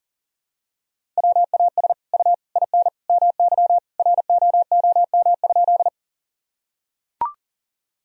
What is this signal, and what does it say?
Morse code, heard as a keyed tone, about 40 wpm, 700 Hz. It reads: WAS U IN MY ROOM?
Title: Sentences from Top 300 Words and Abbreviations 40wpm